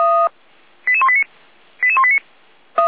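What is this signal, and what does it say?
Five-tone selective call (ZVEI), heard as quick runs of five tones, tones of about 70 ms. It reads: 88189 88189
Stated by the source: SDR-received ZVEI five-tone selective call, code 88352.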